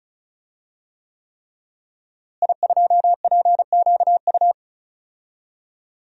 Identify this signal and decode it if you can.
Morse code, heard as a keyed tone, about 34 wpm, 700 Hz. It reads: I2PQU
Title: DXCC-100 Call Signs - 5 Characters - Repeated 3x - 35wpm